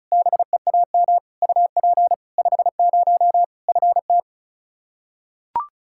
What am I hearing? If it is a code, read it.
Morse code, heard as a keyed tone, about 36 wpm, 700 Hz. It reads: BEAM UP 50 FT
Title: Practice QSO - Exchange 2 - Info - 35wpm